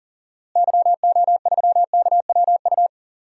Morse code, heard as a keyed tone, about 39 wpm, 700 Hz. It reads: YO3KWU
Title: DXCC-100 Call Signs - 6 Characters - Repeated 2x - 40wpm